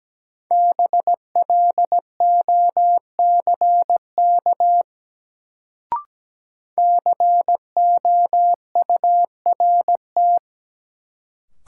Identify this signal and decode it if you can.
Morse code, heard as a keyed tone, about 17 wpm, 700 Hz. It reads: BLOCK COURT
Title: Sentences from Top 1600 Words - New Word Drill Alt - 17wpm